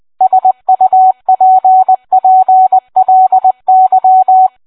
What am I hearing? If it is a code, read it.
Morse code, heard as a keyed tone, about 20 wpm, 750 Hz, with radio noise. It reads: SUPPLY